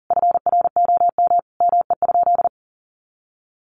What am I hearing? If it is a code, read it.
Morse code, heard as a keyed tone, about 40 wpm, 700 Hz. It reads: FROM ME?
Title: Sentences from Top 1500 Words 40wpm